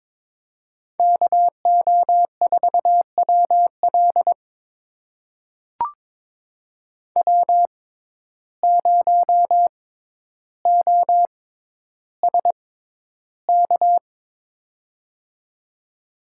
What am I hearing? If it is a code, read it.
Morse code, heard as a keyed tone, about 22 wpm, spaced as for 11 wpm, 700 Hz. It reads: KO4WL W0OSK
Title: USA Call Signs - 5 Characters - 6x Character Spacing - 22wpm